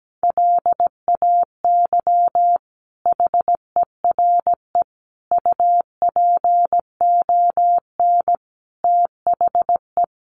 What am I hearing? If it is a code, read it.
Morse code, heard as a keyed tone, about 17 wpm, 700 Hz. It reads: LAY HERE UPON THE
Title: Sentences from Top 1100 Words 17wpm